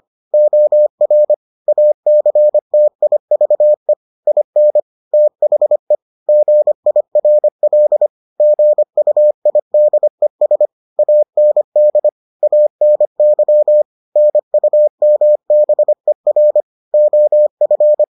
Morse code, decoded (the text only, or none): OR ACTIVE IN THE GIRL GUIDES AND ANY NUMBER OF